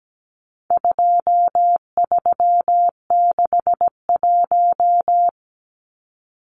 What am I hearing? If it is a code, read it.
Morse code, heard as a keyed tone, about 17 wpm, 700 Hz. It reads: 2361
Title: Numbers - 4-Digit 17wpm